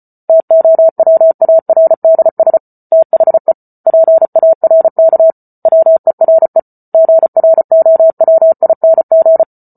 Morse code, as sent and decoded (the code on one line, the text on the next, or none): - --- .-- .- .-. -.. ... / - .... / .--. .- .-. -.- / .-- . .-. / --. .-. --- .-- .. -. --.
TOWARDS THE PARK WERE GROWING